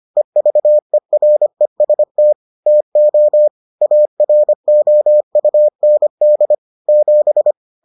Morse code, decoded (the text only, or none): EVEREST TO AROUND 7